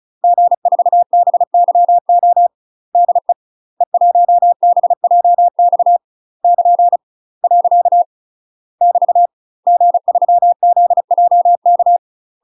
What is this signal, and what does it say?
Morse code, heard as a keyed tone, about 34 wpm, 700 Hz. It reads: G4BYO DE E1BJX <KN> . <BT> G3ZJK
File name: UK MASTER QSOs TRAINING / 35WPM / UK MASTER QSOs 0021